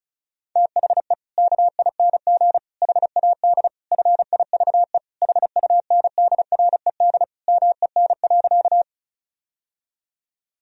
Morse code, heard as a keyed tone, about 34 wpm, 700 Hz. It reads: THE KING HAD FIVE HUNDRED MEN.
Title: Sentences from Top 1000 Words - Review Part 2 - 35wpm